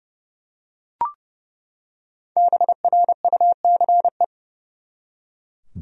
Morse code, heard as a keyed tone, about 29 wpm, 700 Hz. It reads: BRUCE
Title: USA Common Names - 30wpm